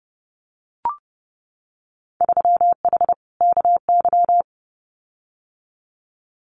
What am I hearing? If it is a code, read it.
Morse code, heard as a keyed tone, about 30 wpm, 700 Hz. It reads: 3H KY
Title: Winter Field Day Exchanges Encore 30wpm